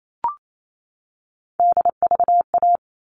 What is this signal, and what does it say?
Morse code, heard as a keyed tone, about 28 wpm, 700 Hz. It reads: DVA
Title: All Cut Numbers - 3-Digit 28wpm